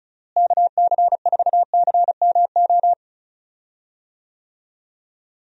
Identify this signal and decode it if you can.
Morse code, heard as a keyed tone, about 36 wpm, 700 Hz. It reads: KC4CMO
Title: USA Call Signs - 6 Characters - Repeated 3x - 35wpm